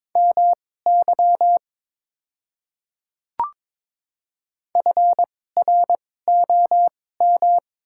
Morse code, heard as a keyed tone, about 22 wpm, spaced as for 16 wpm, 700 Hz. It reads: MY FROM
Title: 2x Character Spacing - Top 100 Words - 22wpm